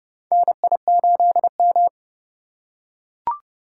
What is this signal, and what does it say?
Morse code, heard as a keyed tone, about 30 wpm, 700 Hz. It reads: NI8M
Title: USA Call Signs - 4 Characters - 8x Character Spacing - 30wpm